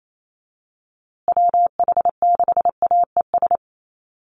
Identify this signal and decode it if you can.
Morse code, heard as a keyed tone, about 28 wpm, 700 Hz. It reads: WH6AES
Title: USA Call Signs Repeated 3x 28wpm